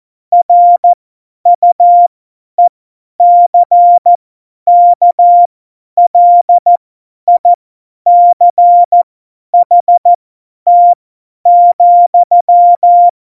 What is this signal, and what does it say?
Morse code, heard as a keyed tone, about 14 wpm, spaced as for 10 wpm, 700 Hz. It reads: RUECKLICHT,
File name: Morsekurs / Witze / Witz 05. Februar 2019, 50 F70